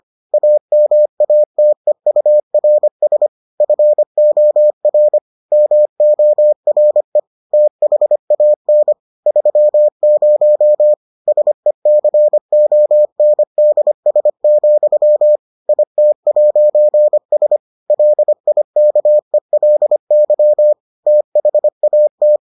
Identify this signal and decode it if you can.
Morse code, heard as a keyed tone, about 25 wpm, 600 Hz. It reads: AMATEURS FOR MORE THAN 30 SECONDS, IT'S LIKELY THAT